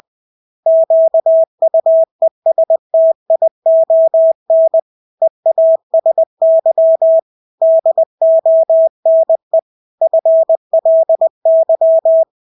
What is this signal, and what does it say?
Morse code, heard as a keyed tone, about 20 wpm, 650 Hz. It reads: QUESTION EASY DONE FLY
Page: Engelske ord 20 wpm